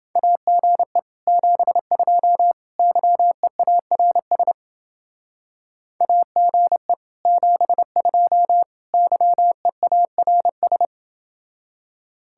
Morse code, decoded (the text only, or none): AGE 72 YEARS AGE 72 YEARS